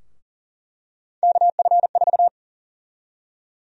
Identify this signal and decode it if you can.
Morse code, heard as a keyed tone, about 37 wpm, 700 Hz. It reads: KF4
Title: USA Call Sign Prefixes - Repeated 2x - 40wpm